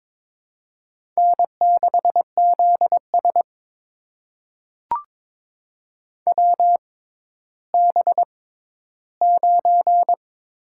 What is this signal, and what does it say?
Morse code, heard as a keyed tone, about 22 wpm, spaced as for 13 wpm, 700 Hz. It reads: N6ZS WB9
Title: USA Call Signs - 4 Characters - 6x Character Spacing - 22wpm